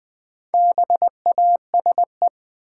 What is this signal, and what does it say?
Morse code, heard as a keyed tone, about 20 wpm, 700 Hz. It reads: BASE
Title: Sentences from Top 1000 Words - Review Part 1 - New Word Drill - 20wpm